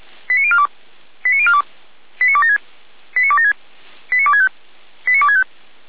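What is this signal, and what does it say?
Five-tone selective call (ZVEI), heard as quick runs of five tones, tones of about 70 ms. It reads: 89042 89042 89276 89276 89265 89265
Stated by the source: ZVEI five-tone call received by SDR.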